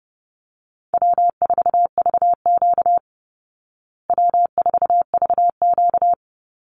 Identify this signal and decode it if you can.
Morse code, heard as a keyed tone, about 30 wpm, 700 Hz. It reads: W4VQ W4VQ
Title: Call Signs Repeated 2x - 30wpm